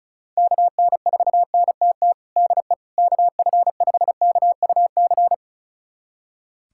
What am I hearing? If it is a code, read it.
Morse code, heard as a keyed tone, about 34 wpm, 700 Hz. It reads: KN4NTT DE KF5KUC
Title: USA Callsign DE Callsign 35wpm